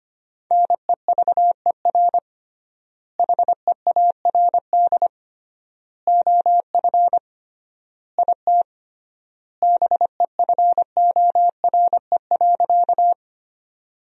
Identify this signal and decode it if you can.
Morse code, heard as a keyed tone, about 25 wpm, 700 Hz. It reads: NEVER HEARD OF IT BEFORE.
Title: 3x Word-Spacing - Sentences from Top 400 Words - 25wpm